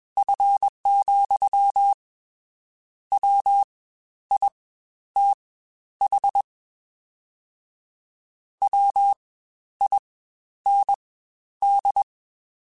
Morse code, heard as a keyed tone, about 21 wpm, spaced as for 11 wpm, 800 Hz. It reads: F, WITH WIND
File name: Spaced Characters / Spaced Characters-mp3s / Weather5 5 sp @20